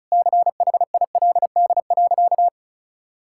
Morse code, heard as a keyed tone, about 34 wpm, 700 Hz. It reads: CHILD.
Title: Sentences from Top 1000 Words - Review Part 5 - 35wpm